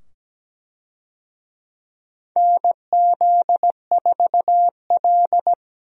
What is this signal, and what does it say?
Morse code, heard as a keyed tone, about 17 wpm, 700 Hz. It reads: NZ4L
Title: USA Call Signs - 4 Characters - 17wpm